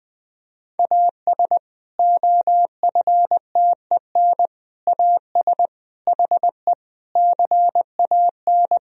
Morse code, heard as a keyed tone, about 20 wpm, 700 Hz. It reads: AS OFTEN AS HE CAN
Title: Sentences from Top 1000 Words - Review Part 1 - 20wpm